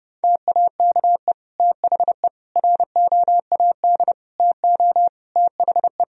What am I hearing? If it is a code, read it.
Morse code, heard as a keyed tone, about 30 wpm, 700 Hz. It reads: TAKE THE ROAD TO THE
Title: Sentences from Top 1000 Words - Review Part 2 - 30wpm